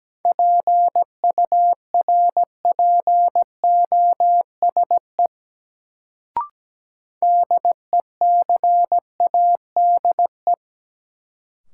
Morse code, heard as a keyed tone, about 17 wpm, 700 Hz. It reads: PURPOSE DECADE
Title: Sets of 5 Words - New Word Drill - 17wpm